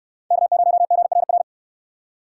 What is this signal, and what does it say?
Morse code, heard as a keyed tone, about 69 wpm, 700 Hz. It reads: K0GRR